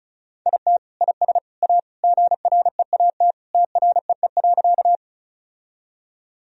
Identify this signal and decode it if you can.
Morse code, heard as a keyed tone, about 36 wpm, 700 Hz. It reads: IT IS A GREAT TREE.
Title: Sentences from Top 1000 Words - Review Part 1 - 35wpm